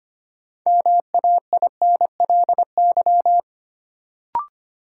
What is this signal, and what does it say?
Morse code, heard as a keyed tone, about 25 wpm, 700 Hz. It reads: MAINLY
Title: Sentences from Top 1400 Words - New Word Drill - 25wpm